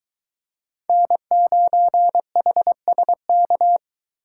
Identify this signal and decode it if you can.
Morse code, heard as a keyed tone, about 23 wpm, 700 Hz. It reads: N9HSK